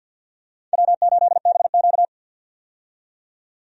Morse code, heard as a keyed tone, about 51 wpm, 700 Hz. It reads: W8BQ
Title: USA Call Signs - 4 Characters - 50wpm